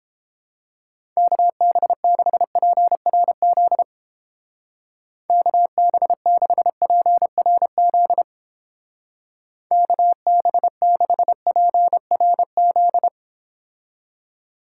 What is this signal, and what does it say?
Morse code, heard as a keyed tone, about 31 wpm, 700 Hz. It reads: KB6PRZ KB6PRZ KB6PRZ